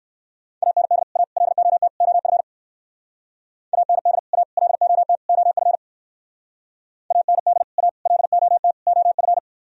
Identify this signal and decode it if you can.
Morse code, heard as a keyed tone, about 54 wpm, 700 Hz. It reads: AND A LOT OF AND A LOT OF AND A LOT OF
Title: Speed-Racing - Sets of 4 Words 45wpm